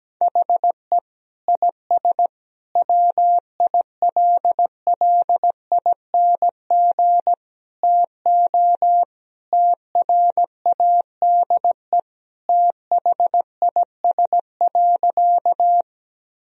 Morse code, heard as a keyed tone, about 17 wpm, 700 Hz. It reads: HE IS WILLING TO TRADE THIS.
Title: Sentences from Top 1000 Words - Review Part 4 - 17wpm